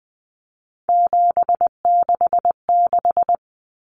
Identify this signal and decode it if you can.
Morse code, heard as a keyed tone, about 20 wpm, 700 Hz. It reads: 766
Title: Numbers - 3-Digit 20wpm